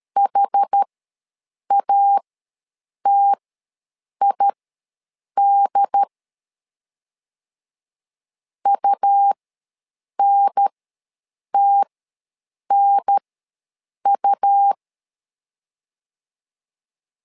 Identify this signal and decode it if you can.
Morse code, heard as a keyed tone, about 13 wpm, spaced as for 7 wpm, 800 Hz. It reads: HATID UNTNU
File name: LEZIONI / 2 UHDN / 2 UHDN ws 12wpm 1708